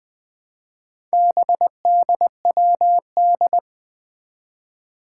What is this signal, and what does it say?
Morse code, heard as a keyed tone, about 21 wpm, 700 Hz. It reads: BDWD
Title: Four Challenging Letter-Number Combinations 20wpm